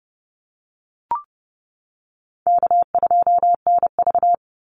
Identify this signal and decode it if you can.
Morse code, heard as a keyed tone, about 30 wpm, 700 Hz. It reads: K2NV
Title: Easy Call Signs Repeated 3x - 30wpm